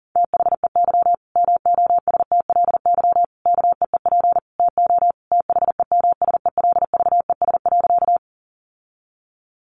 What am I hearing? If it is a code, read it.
Morse code, heard as a keyed tone, about 40 wpm, 700 Hz. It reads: THEY MOSTLY KEEP TO THEMSELVES.
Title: Sentences from Top 1300 Words 40wpm